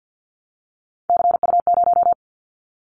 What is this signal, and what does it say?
Morse code, heard as a keyed tone, about 50 wpm, 700 Hz. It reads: CU0
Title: Callsign Prefixes for Top 50 DXCC - 50wpm